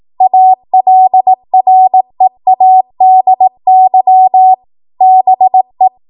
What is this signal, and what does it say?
Morse code, heard as a keyed tone, about 18 wpm, 750 Hz. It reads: ALREADY BE
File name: Morse / 18WPM